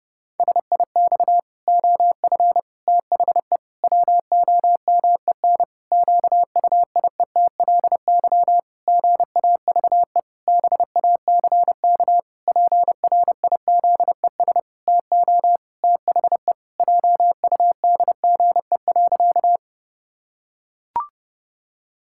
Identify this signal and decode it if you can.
Morse code, heard as a keyed tone, about 30 wpm, 700 Hz. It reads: SIX OF THE WOMEN QUIETLY GAVE BACK PRIZES TO THE JUDGE.
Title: Pangram Challenge v2 - 30wpm